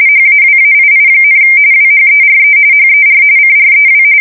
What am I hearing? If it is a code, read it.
RTTY, about 45.45 baud. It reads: HAWAII, WHERE HER CREW
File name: fsk170 / USS-New-Jersey